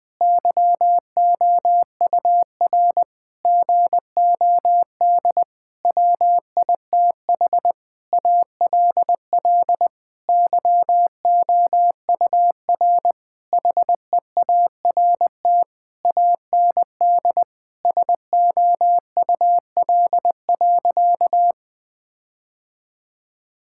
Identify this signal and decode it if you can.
Morse code, heard as a keyed tone, about 20 wpm, 700 Hz. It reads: YOUR GOD WITH ALL YOUR HEART AND SOUL.
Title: Sentences with 9-Word Phrases - Spoken Beginning - 20wpm